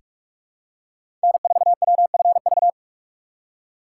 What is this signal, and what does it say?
Morse code, heard as a keyed tone, about 47 wpm, 700 Hz. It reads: N4WFV
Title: USA Call Signs - 3x Character Spacing - 45wpm